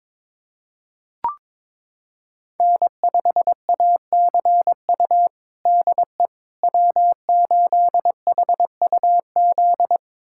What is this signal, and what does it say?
Morse code, heard as a keyed tone, about 22 wpm, 700 Hz. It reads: N5ACU DE W8HUZ
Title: USA Callsign DE Callsign - 5 Characters - 22wpm